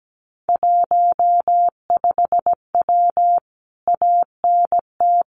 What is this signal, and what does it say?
Morse code, heard as a keyed tone, about 17 wpm, 700 Hz. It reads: 15W ANT
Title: Common QSO Phrases Encore 17wpm